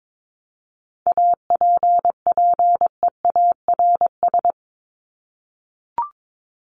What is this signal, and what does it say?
Morse code, heard as a keyed tone, about 22 wpm, 700 Hz. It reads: APPEARS
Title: ES:S Suffix - Top 100 Words with Base Words 22wpm